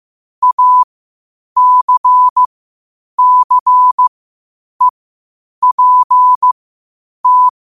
Morse code, heard as a keyed tone, about 15 wpm, spaced as for 8 wpm, 1000 Hz. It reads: ACCEPT
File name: qotdfiles / 8wpm